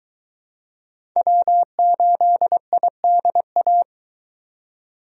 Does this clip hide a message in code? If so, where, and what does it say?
Yes: Morse code from 1.2 to 3.8 s, heard as a keyed tone, about 24 wpm, 700 Hz. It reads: W8IDA